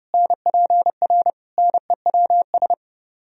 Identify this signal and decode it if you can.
Morse code, heard as a keyed tone, about 25 wpm, 700 Hz. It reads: NPR NEWS
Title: Speed-Racing - Sets of 5 Words 25wpm